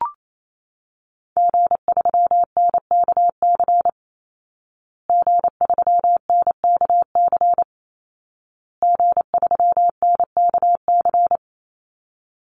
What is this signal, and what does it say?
Morse code, heard as a keyed tone, about 28 wpm, 700 Hz. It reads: G3NKC G3NKC G3NKC
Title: Call Signs Repeated 2x - 28wpm